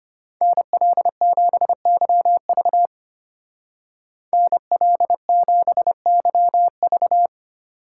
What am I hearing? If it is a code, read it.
Morse code, heard as a keyed tone, about 31 wpm, 700 Hz. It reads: NL7YV NL7YV
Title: Speed Racing - USA 2x2 Call Signs 25wpm